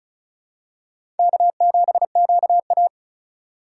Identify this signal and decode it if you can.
Morse code, heard as a keyed tone, about 34 wpm, 700 Hz. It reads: K7QA
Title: USA Call Signs - 4 Characters - 4x Character Spacing - 35wpm